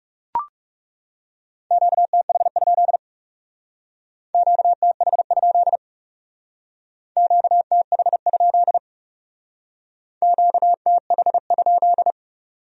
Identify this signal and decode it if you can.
Morse code, heard as a keyed tone, about 36 wpm, 700 Hz. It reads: QTH? QTH? QTH? QTH?
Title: Speed-Racing Q-Codes - 30wpm